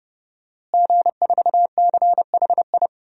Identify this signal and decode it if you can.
Morse code, heard as a keyed tone, about 32 wpm, 700 Hz. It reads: G4CHI
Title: DXCC-100 Call Signs - 5 Characters - Repeated 2x - 30wpm